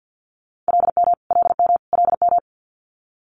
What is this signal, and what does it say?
Morse code, heard as a keyed tone, about 50 wpm, 700 Hz. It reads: LM LM LM
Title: Multi-letter Phonemes Repeated 3x 50wpm